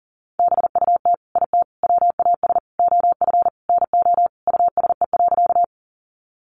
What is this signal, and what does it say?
Morse code, heard as a keyed tone, about 37 wpm, 700 Hz. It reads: BUT IT WAS OF NO USE.
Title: Sentences from Top 100 Words 40wpm